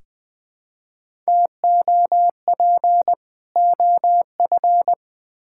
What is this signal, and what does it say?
Morse code, heard as a keyed tone, about 21 wpm, 700 Hz. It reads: TOP OF
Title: Build A Phrase - 3-Word Phrases - 20wpm